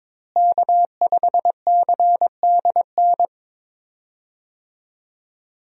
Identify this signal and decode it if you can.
Morse code, heard as a keyed tone, about 22 wpm, 700 Hz. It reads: K5CDN
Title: USA Call Signs - 5 Characters - Repeated 3x - 22wpm